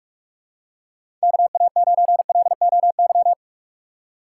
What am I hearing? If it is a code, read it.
Morse code, heard as a keyed tone, about 46 wpm, 700 Hz. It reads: KA9LOY
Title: USA Call Signs - 6 Characters - 4x Character Spacing - 45wpm